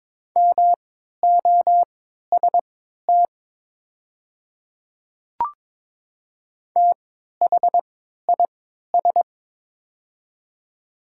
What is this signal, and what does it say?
Morse code, heard as a keyed tone, about 22 wpm, spaced as for 13 wpm, 700 Hz. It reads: MOST THIS
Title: 3x Character Spacing - Top 100 Words - 22wpm